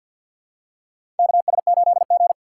Morse code, heard as a keyed tone, about 52 wpm, 700 Hz. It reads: KS8G
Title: USA Call Signs - 4 Characters - 6x Character Spacing - 50wpm